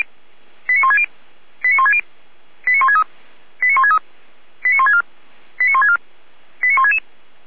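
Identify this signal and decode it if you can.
Five-tone selective call (ZVEI), heard as quick runs of five tones, tones of about 80 ms. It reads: 89160 89160 89163 89163 89164 89164 89166
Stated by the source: ZVEI five-tone selective call received by SDR.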